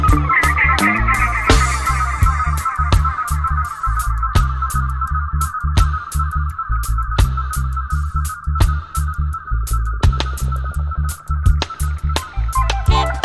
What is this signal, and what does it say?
There is no code.